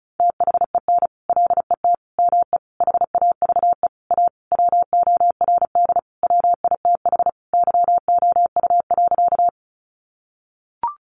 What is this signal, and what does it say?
Morse code, heard as a keyed tone, about 34 wpm, 700 Hz. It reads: THEN LET ME HAVE A WORD WITH YOU.
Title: Sentences from Top 300 Words 35wpm